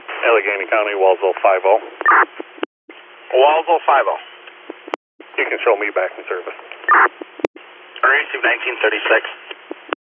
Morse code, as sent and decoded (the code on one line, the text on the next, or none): none